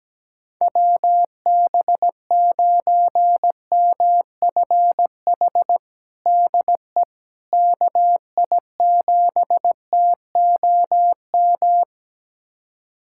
Morse code, heard as a keyed tone, about 17 wpm, 700 Hz. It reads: WB9MFH DE KI7TOM
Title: USA Callsign DE Callsign - 6 Characters - 17wpm